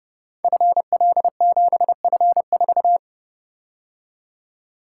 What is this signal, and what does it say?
Morse code, heard as a keyed tone, about 28 wpm, 700 Hz. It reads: FL7F4
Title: Code Groups - 5 Letter:Numbers - 30wpm